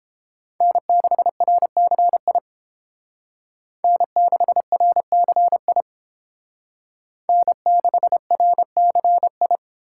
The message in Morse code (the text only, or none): N6RCI N6RCI N6RCI